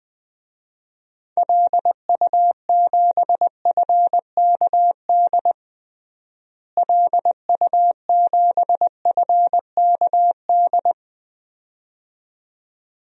Morse code, heard as a keyed tone, about 20 wpm, 700 Hz. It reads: LU7FKD LU7FKD
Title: DXCC-100 Call Signs - 6 Characters - Repeated 2x - 20wpm